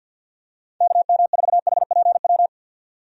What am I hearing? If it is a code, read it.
Morse code, heard as a keyed tone, about 50 wpm, 700 Hz. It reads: KM4HPW
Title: USA Call Signs - 6 Characters - 50wpm